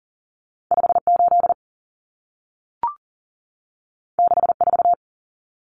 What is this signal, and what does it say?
Morse code, heard as a keyed tone, about 40 wpm, 700 Hz. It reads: 58 64
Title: Common Cut Numbers - 2-Digit 40wpm